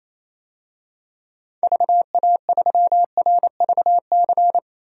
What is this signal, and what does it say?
Morse code, heard as a keyed tone, about 28 wpm, 700 Hz. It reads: VA3RVC
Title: DXCC-100 Call Signs - 6 Characters - 28wpm